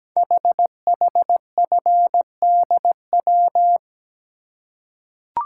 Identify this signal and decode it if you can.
Morse code, heard as a keyed tone, about 17 wpm, 700 Hz. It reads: HHFDW
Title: Code Groups - 5 Letters - 17wpm